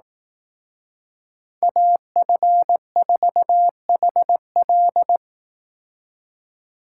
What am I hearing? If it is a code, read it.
Morse code, heard as a keyed tone, about 18 wpm, 700 Hz. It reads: AF4HL
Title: Speed Racing - USA 2x2 Call Signs 15wpm